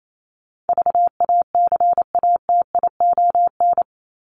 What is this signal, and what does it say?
Morse code, heard as a keyed tone, about 28 wpm, 700 Hz. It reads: VACATION
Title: Top 1501-1750 Words 28wpm